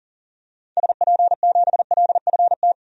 Morse code, heard as a keyed tone, about 40 wpm, 700 Hz. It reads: SP7LFT
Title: DXCC-100 Call Signs - 6 Characters - 40wpm